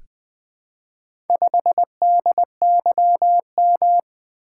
Morse code, heard as a keyed tone, about 20 wpm, 700 Hz. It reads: 5DYM
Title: Four Letter-Number Combinations - 20wpm